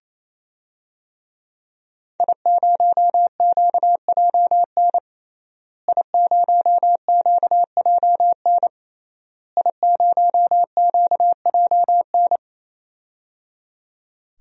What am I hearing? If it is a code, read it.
Morse code, heard as a keyed tone, about 28 wpm, 700 Hz. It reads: I0QJN I0QJN I0QJN
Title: DXCC-100 Call Signs - 5 Characters - Repeated 3x - 28wpm